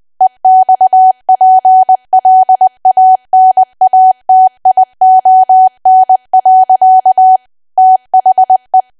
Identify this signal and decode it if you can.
Morse code, heard as a keyed tone, about 20 wpm, 750 Hz. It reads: EXPLANATION. THE
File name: cwa / W1AW-Level3 / 20WPM